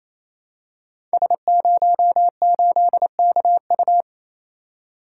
Morse code, heard as a keyed tone, about 29 wpm, 700 Hz. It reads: S08KU
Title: Code Groups - 5 Letter:Numbers - 28wpm